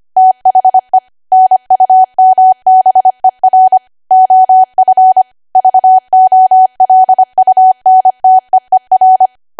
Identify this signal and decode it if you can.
Morse code, heard as a keyed tone, about 25 wpm, 750 Hz. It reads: THE NUMBER OF VOLUNTEER